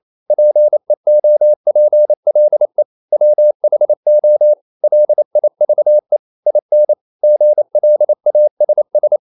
Morse code, heard as a keyed tone, about 28 wpm, 600 Hz. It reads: PEOPLE WHO LIVE IN GLASS
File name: CW / Practice / EnglishSayings28